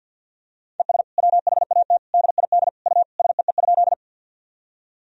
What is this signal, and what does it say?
Morse code, heard as a keyed tone, about 47 wpm, 700 Hz. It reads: ES WHAT DID U SEE?
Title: Sentences from Top 100 Words and Abbreviations 50wpm